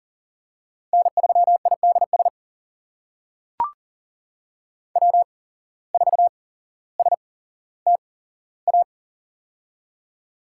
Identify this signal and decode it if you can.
Morse code, heard as a keyed tone, about 41 wpm, spaced as for 17 wpm, 700 Hz. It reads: N3IDS W4STA
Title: USA Call Signs - 5 Characters - 8x Character Spacing - 40wpm